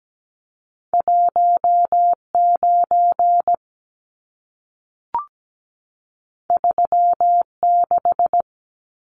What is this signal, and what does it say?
Morse code, heard as a keyed tone, about 17 wpm, 700 Hz. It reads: 19 36
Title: Numbers - 2-Digit 17wpm